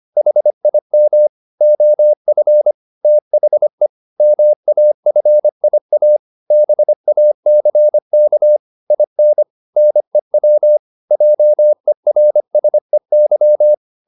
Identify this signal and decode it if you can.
Morse code, heard as a keyed tone, about 25 wpm, 600 Hz. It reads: HIM OF THE MAFIA BACK IN NEW JERSEY
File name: Never 0025wpm